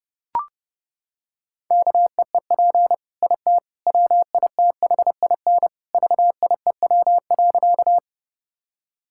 Morse code, heard as a keyed tone, about 30 wpm, 700 Hz. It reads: KEEP IT WITHIN VIEW.
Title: Sentences from Top 1000 Words - Review Part 6 - 30wpm